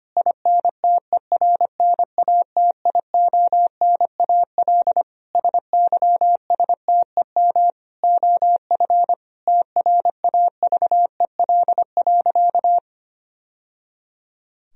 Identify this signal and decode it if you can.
Morse code, heard as a keyed tone, about 25 wpm, 700 Hz. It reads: INTERNATIONAL SYSTEM OF TRAVEL.
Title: Sentences from Top 1000 Words - Review Part 4 - 25wpm